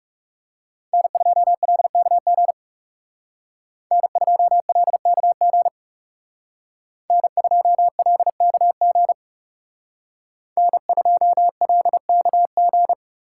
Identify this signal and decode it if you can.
Morse code, heard as a keyed tone, about 38 wpm, 700 Hz. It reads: N2LKG N2LKG N2LKG N2LKG